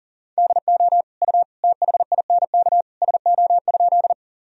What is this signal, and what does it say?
Morse code, heard as a keyed tone, about 37 wpm, 700 Hz. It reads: DO U THINK SO?
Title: Sentences from Top 200 Words and Abbreviations 40wpm